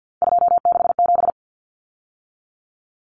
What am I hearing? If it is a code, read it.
Morse code, heard as a keyed tone, about 50 wpm, 700 Hz. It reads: 267
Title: Numbers - 3-Digit 50wpm